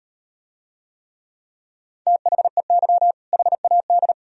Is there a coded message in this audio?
Yes: Morse code from 2.1 to 4.1 s, heard as a keyed tone, about 38 wpm, 700 Hz. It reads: THEY HAD